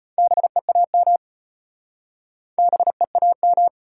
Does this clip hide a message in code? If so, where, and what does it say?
Yes: Morse code from 0.2 to 3.7 s, heard as a keyed tone, about 34 wpm, 700 Hz. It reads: BEAM BEAM